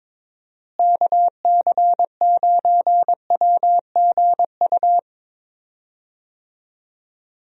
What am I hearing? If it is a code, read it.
Morse code, heard as a keyed tone, about 22 wpm, 700 Hz. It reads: KC9WGU
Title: USA Call Signs - 6 Characters - Repeated 3x - 22wpm